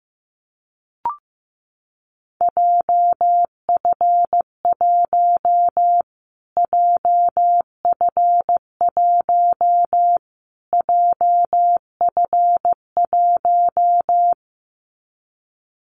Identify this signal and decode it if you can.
Morse code, heard as a keyed tone, about 15 wpm, 700 Hz. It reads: JF1 JF1 JF1
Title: Callsign Prefixes for Top 25 DXCC - Repeated 3x - 15wpm